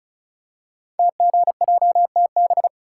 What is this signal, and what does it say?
Morse code, heard as a keyed tone, about 36 wpm, 690 Hz. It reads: TGJTB
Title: Code Groups - 5 Letters - 35wpm